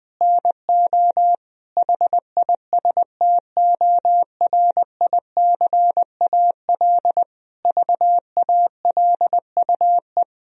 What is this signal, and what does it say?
Morse code, heard as a keyed tone, about 20 wpm, 700 Hz. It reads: NO HISTORICAL VALUE
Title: Sentences from Top 1000 Words - Review Part 4 - 20wpm